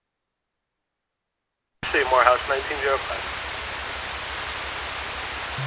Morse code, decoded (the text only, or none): none